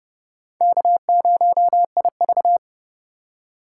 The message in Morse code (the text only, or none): K0IV